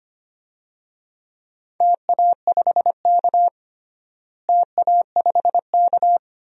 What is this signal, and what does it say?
Morse code, heard as a keyed tone, about 25 wpm, 700 Hz. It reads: TA5K TA5K
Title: DXCC-100 Call Signs - 4 Characters - Repeated 2x - 25wpm